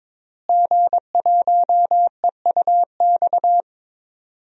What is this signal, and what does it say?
Morse code, heard as a keyed tone, about 22 wpm, 700 Hz. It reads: G1EUX
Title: DXCC-100 Call Signs - 5 Characters - Repeated 2x - 22wpm